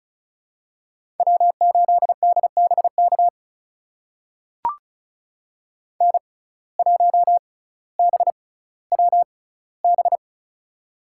Morse code, heard as a keyed tone, about 36 wpm, spaced as for 18 wpm, 700 Hz. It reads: W8DBK N1BWB
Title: USA Call Signs - 5 Characters - 6x Character Spacing - 35wpm